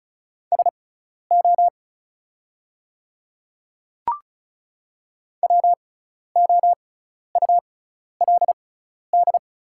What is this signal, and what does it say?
Morse code, heard as a keyed tone, about 36 wpm, spaced as for 12 wpm, 700 Hz. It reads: SO WOULD
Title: 6x Character Spacing - Top 100 Words - 35wpm